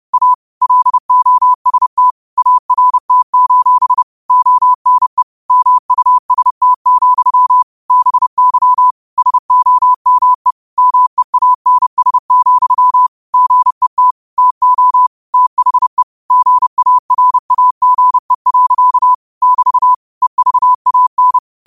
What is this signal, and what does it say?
Morse code, heard as a keyed tone, about 30 wpm, 1000 Hz. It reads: A LOST ART: ONE MUST, BY SOME MEANS, GET TO THE GARAGE. <BT> EVAN
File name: qotdfiles / 30wpm